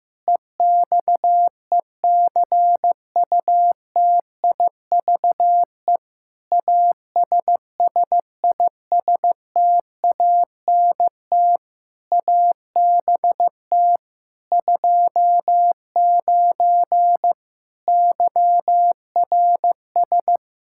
Morse code, heard as a keyed tone, about 15 wpm, 700 Hz. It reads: EXECUTIVE ASSISTANT ABT 29 YRS
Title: Practice QSO - Exchange 3 - More Info - 15wpm